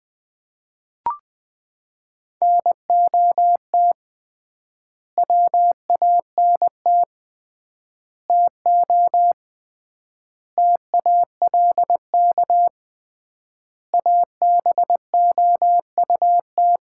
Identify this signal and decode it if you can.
Morse code, heard as a keyed tone, about 21 wpm, 700 Hz. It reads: NOT WANT TO TALK ABOUT
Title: 3x Word-Spacing - Sets of 5 Words - 20wpm